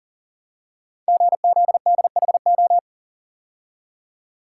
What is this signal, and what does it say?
Morse code, heard as a keyed tone, about 38 wpm, 690 Hz. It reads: GZDHO
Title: Code Groups - 5 Letters - 40wpm